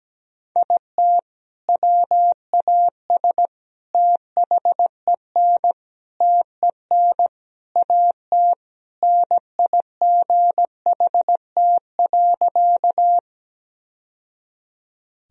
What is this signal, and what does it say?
Morse code, heard as a keyed tone, about 17 wpm, 700 Hz. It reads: IT WAS THEN TEN AT NIGHT.
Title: Sentences from Top 1000 Words - Review Part 2 - 17wpm